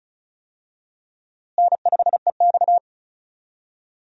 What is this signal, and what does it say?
Morse code, heard as a keyed tone, about 36 wpm, 700 Hz. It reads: N5EX
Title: USA Call Signs - 4 Characters - Repeated 2x - 35wpm